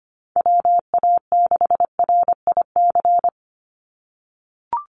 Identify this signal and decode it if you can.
Morse code, heard as a keyed tone, about 25 wpm, 700 Hz. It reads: WA6RIC
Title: NA SOTA Chaser Callsigns 25wpm